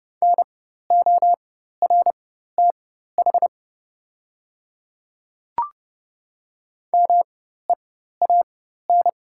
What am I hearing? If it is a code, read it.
Morse code, heard as a keyed tone, about 30 wpm, spaced as for 14 wpm, 700 Hz. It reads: NORTH MEAN
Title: 4x Character Spacing - Top 300 Words - 30wpm